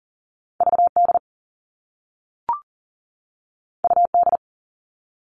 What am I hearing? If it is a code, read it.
Morse code, heard as a keyed tone, about 42 wpm, 700 Hz. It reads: VD UD
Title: All Cut Numbers - 2-Digit 40wpm